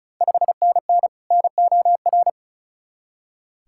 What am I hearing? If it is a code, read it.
Morse code, heard as a keyed tone, about 36 wpm, 690 Hz. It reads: 5NN NOR
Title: Ontario QSO Party Exchange 35wpm